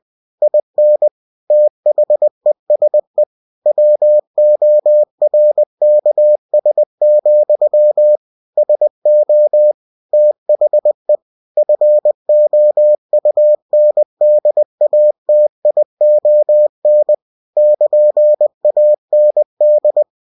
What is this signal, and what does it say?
Morse code, heard as a keyed tone, about 20 wpm, 600 Hz. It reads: IN THESE WORKS, SO THE FOUNDATION <KN>AND
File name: cw-books / winnie-the-pooh / winnie-20wpm-page-0112